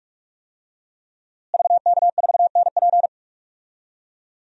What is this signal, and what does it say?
Morse code, heard as a keyed tone, about 43 wpm, 700 Hz. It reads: VK4NP
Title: DXCC-100 Call Signs - 5 Characters - Repeated 3x - 45wpm